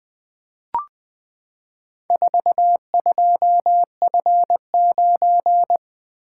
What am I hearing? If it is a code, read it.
Morse code, heard as a keyed tone, about 21 wpm, 700 Hz. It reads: 42F9
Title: Four Letter-Number Combinations - 20wpm